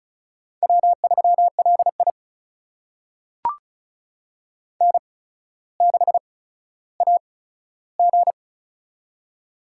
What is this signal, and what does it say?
Morse code, heard as a keyed tone, about 36 wpm, spaced as for 15 wpm, 700 Hz. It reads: W3LI N6AG
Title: USA Call Signs - 8x Character Spacing - 35wpm